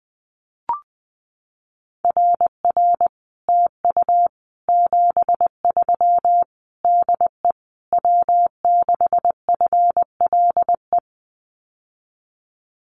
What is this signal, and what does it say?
Morse code, heard as a keyed tone, about 20 wpm, 700 Hz. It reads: RR TU 73 DE W6FLE
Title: Common QSO Phrases Encore 20wpm